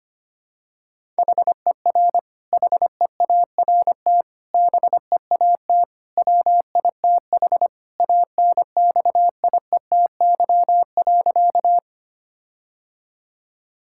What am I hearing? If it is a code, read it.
Morse code, heard as a keyed tone, about 25 wpm, 700 Hz. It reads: HER HEART BEAT WITH ANXIETY.
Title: Sentences from Top 1600 Words 25wpm